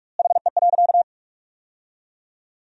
Morse code, heard as a keyed tone, about 46 wpm, 700 Hz. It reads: HE.
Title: 4x Word-Spacing - Sentences from Top 100 Words - 45wpm